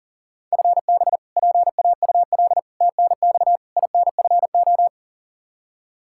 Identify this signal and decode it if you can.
Morse code, heard as a keyed tone, about 37 wpm, 700 Hz. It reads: FB PAUL TNX INFO
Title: Practice QSO - Exchange 3 - More Info - 40wpm